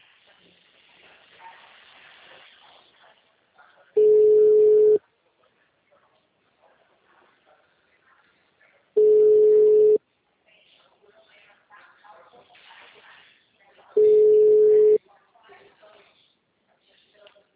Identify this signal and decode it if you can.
Ringback tone (Europe, 425 Hz), with speech.